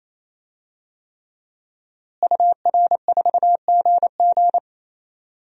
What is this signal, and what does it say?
Morse code, heard as a keyed tone, about 28 wpm, 700 Hz. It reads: UR4GG
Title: DXCC-100 Call Signs - 5 Characters - Repeated 3x - 28wpm